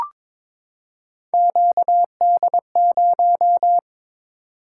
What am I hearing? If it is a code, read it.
Morse code, heard as a keyed tone, about 22 wpm, 700 Hz. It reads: QD0
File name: Three Letter-Number Combinations - 22wpm